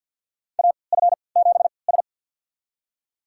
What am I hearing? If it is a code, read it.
Morse code, heard as a keyed tone, about 46 wpm, spaced as for 27 wpm, 700 Hz. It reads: AF7S
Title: USA Call Signs - 3x Character Spacing - 50wpm